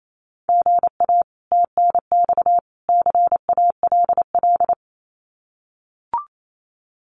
Morse code, heard as a keyed tone, about 28 wpm, 700 Hz. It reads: GA TNX CALL
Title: Common QSO Phrases 28wpm